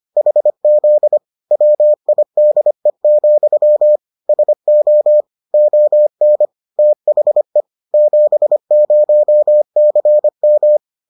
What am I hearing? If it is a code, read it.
Morse code, heard as a keyed tone, about 25 wpm, 600 Hz. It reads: HZ WIDE, SO ON THE 70CM